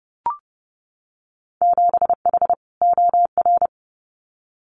Morse code, heard as a keyed tone, about 30 wpm, 700 Hz. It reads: 7H OR
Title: Winter Field Day Exchanges Encore 30wpm